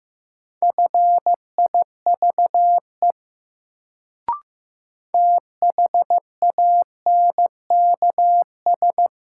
Morse code, heard as a keyed tone, about 15 wpm, 700 Hz. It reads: FIVE THANKS
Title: Sets of 5 Words - New Word Drill - 15wpm